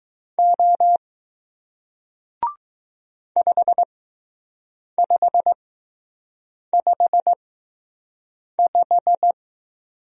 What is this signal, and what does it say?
Morse code, heard as a keyed tone, about 20 wpm, spaced as for 6 wpm, 700 Hz. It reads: O 5555